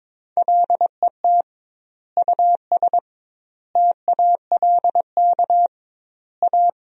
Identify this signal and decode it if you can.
Morse code, heard as a keyed tone, about 22 wpm, 700 Hz. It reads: LET US TALK A
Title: 2x Word-Spacing - Sentences from Top 400 Words - 22wpm